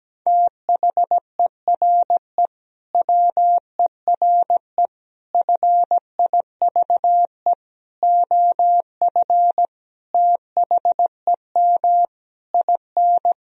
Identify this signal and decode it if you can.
Morse code, heard as a keyed tone, about 17 wpm, 700 Hz. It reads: THERE WERE FIVE OF THEM IN
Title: Sentences from Top 1000 Words - Review Part 2 - 17wpm